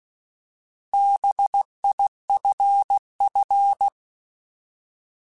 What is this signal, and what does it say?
Morse code, heard as a keyed tone, about 16 wpm, 800 Hz. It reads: BIFF
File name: Alphabetical / Alphabetical mp3s / Bs @15